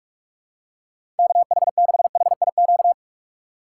Morse code, heard as a keyed tone, about 46 wpm, 700 Hz. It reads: KH6HIQ
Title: USA Call Signs - 6 Characters - 3x Character Spacing - 45wpm